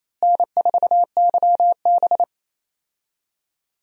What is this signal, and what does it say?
Morse code, heard as a keyed tone, about 28 wpm, 700 Hz. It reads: N4YB
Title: USA Call Signs - 4 Characters - Repeated 3x - 28wpm